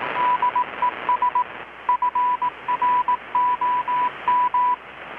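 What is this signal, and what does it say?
Morse code, heard as a keyed tone, about 18 wpm, 990 Hz, with radio noise. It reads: DES FROM
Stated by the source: Morse from KPH's Night of Nights XX maritime radiotelegraphy event, received in Chiapas, Mexico about 3,800 km away on a Kenwood R-600 with a grounded 10 m long-wire antenna.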